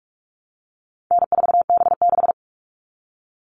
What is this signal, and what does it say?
Morse code, heard as a keyed tone, about 44 wpm, 700 Hz. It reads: N4B6
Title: All Cut Numbers - 4-Digit 45wpm